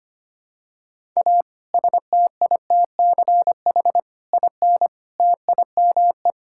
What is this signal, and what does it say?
Morse code, heard as a keyed tone, about 25 wpm, 700 Hz. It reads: A STITCH IN TIME
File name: practice / 25wpm / Difficult Sentences with Hints - 25wpm